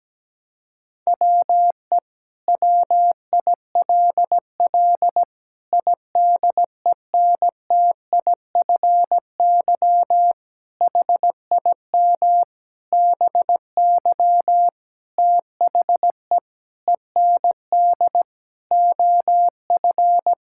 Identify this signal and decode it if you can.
Morse code, heard as a keyed tone, about 17 wpm, 700 Hz. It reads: WE WILL IDENTIFY HIM BY THE END OF
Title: Sentences from Top 1000 Words - Review Part 6 - 17wpm